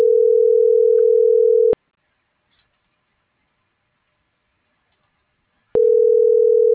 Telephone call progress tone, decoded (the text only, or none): ringback tone